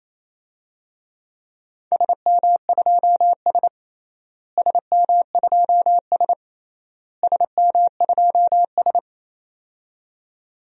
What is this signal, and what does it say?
Morse code, heard as a keyed tone, about 28 wpm, 700 Hz. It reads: SM2S SM2S SM2S
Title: DXCC-100 Call Signs - 4 Characters - Repeated 3x - 28wpm